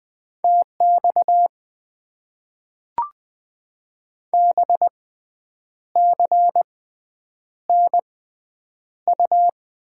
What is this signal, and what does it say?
Morse code, heard as a keyed tone, about 20 wpm, spaced as for 8 wpm, 700 Hz. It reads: TX BCNU
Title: Top 90 QSO Elements - 6x Character Spacing - 20wpm